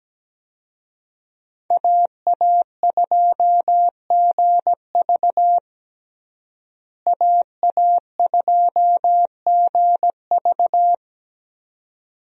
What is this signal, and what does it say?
Morse code, heard as a keyed tone, about 17 wpm, 700 Hz. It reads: AA2GV AA2GV
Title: USA Call Signs - 5 Characters - Repeated 2x - 17wpm